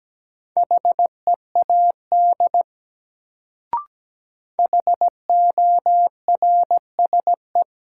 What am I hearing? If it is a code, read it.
Morse code, heard as a keyed tone, about 17 wpm, 700 Hz. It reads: HEAD HORSE
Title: Sentences from Top 300 Words - New Word Drill - 17wpm